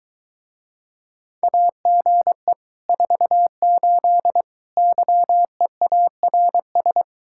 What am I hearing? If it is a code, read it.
Morse code, heard as a keyed tone, about 23 wpm, 700 Hz. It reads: AGE 48 YEARS